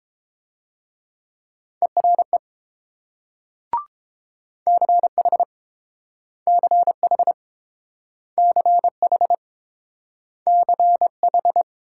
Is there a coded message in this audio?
Yes: Morse code from 1.8 to 11.6 s, heard as a keyed tone, about 29 wpm, 700 Hz. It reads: ERE CH CH CH CH